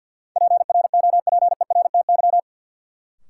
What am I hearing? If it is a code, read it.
Morse code, heard as a keyed tone, about 52 wpm, 700 Hz. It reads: PROPERTY
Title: Sentences from Top 1000 Words - New Word Drill - 50wpm